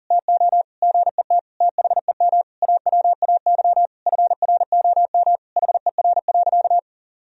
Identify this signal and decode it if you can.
Morse code, heard as a keyed tone, about 37 wpm, 700 Hz. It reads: TO GET THEM AWAY FROM HER.
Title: Sentences from Top 1000 Words - Review Part 2 - 40wpm